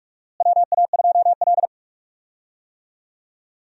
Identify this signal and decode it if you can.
Morse code, heard as a keyed tone, about 43 wpm, 700 Hz. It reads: WA2L
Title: USA Call Signs - 4 Characters - Repeated 2x - 45wpm